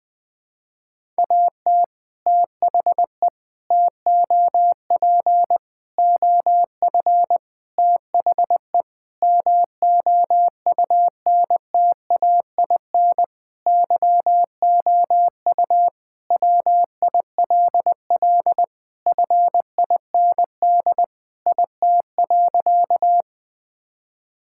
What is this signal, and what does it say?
Morse code, heard as a keyed tone, about 20 wpm, 700 Hz. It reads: AT THE TOP OF THE MOUNTAIN YOU WILL FIND IT.
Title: Sentences with 3-Word Phrases - Spoken Beginning - 20wpm